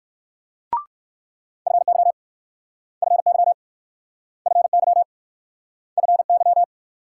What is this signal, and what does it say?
Morse code, heard as a keyed tone, about 63 wpm, 700 Hz. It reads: FY FY FY FY